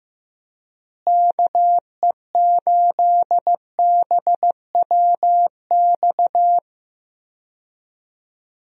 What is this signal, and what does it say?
Morse code, heard as a keyed tone, about 15 wpm, 700 Hz. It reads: KE8BWX
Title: USA Call Signs - 6 Characters - Repeated 3x - 15wpm